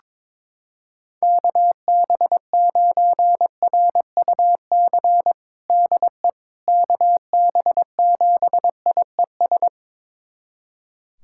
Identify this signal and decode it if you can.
Morse code, heard as a keyed tone, about 22 wpm, 700 Hz. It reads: KB9RUC DE KB7IES
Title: USA Callsign DE Callsign - 6 Characters - 22wpm